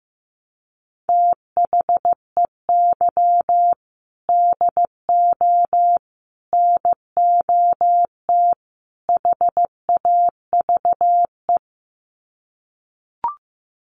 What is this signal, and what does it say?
Morse code, heard as a keyed tone, about 15 wpm, 700 Hz. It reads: THEY DO NOT HAVE
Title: Sets of 4 Words 15wpm